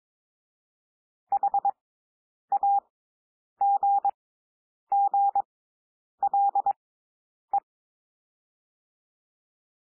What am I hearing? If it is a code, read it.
Morse code, heard as a keyed tone, about 22 wpm, spaced as for 9 wpm, 800 Hz. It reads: HAGGLE
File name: SOUND WORDS 22 wpm